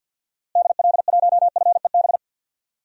Morse code, heard as a keyed tone, about 51 wpm, 700 Hz. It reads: DL1FEB
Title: DXCC-100 Call Signs - 6 Characters - Repeated 3x - 50wpm